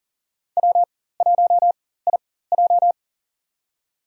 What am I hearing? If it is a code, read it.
Morse code, heard as a keyed tone, about 40 wpm, spaced as for 19 wpm, 700 Hz. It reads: W1IJ
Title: USA Call Signs - 4 Characters - 4x Character Spacing - 40wpm